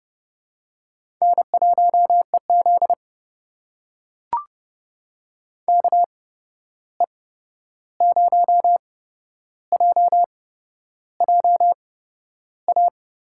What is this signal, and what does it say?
Morse code, heard as a keyed tone, about 30 wpm, spaced as for 11 wpm, 700 Hz. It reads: N1EZ KE0JJA
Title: USA Call Signs - 8x Character Spacing - 30wpm